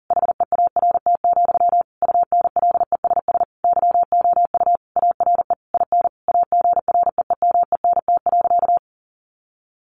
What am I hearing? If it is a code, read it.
Morse code, heard as a keyed tone, about 40 wpm, 700 Hz. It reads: HEART, UNLESS YOU ARE IN AGREEMENT.
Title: Sentences from Top 1500 Words 40wpm